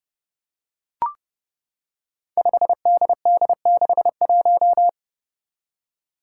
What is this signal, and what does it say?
Morse code, heard as a keyed tone, about 30 wpm, 700 Hz. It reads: 5DD61
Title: Five Challenging Letter-Number Combinations 30wpm